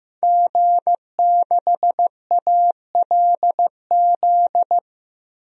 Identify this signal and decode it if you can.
Morse code, heard as a keyed tone, about 15 wpm, 700 Hz. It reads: G6ALZ
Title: DXCC-100 Call Signs - 5 Characters - Repeated 2x - 15wpm